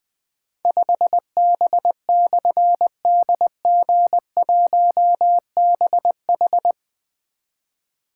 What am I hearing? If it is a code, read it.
Morse code, heard as a keyed tone, about 21 wpm, 700 Hz. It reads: 5B/DG1BH
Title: Difficult Call Signs 20wpm